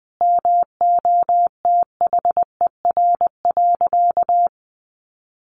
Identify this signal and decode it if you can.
Morse code, heard as a keyed tone, about 20 wpm, 700 Hz. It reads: MOTHER.
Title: Sentences from Top 400 Words 20wpm